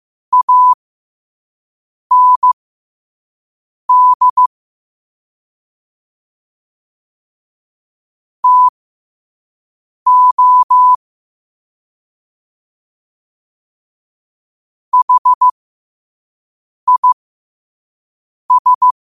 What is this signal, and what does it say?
Morse code, heard as a keyed tone, about 15 wpm, spaced as for 5 wpm, 1000 Hz. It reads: AND TO HIS
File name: qotdfiles / 5wpm